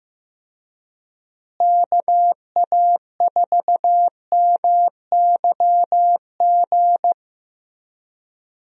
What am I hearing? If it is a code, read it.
Morse code, heard as a keyed tone, about 15 wpm, 700 Hz. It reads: KA4MYG
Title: USA Call Signs - 6 Characters - Repeated 3x - 15wpm